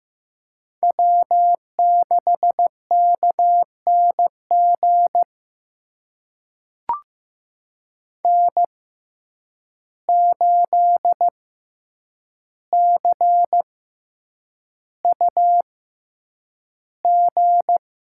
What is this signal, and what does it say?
Morse code, heard as a keyed tone, about 15 wpm, spaced as for 8 wpm, 700 Hz. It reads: W6KNG N8CUG
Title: USA Call Signs - 5 Characters - 6x Character Spacing - 15wpm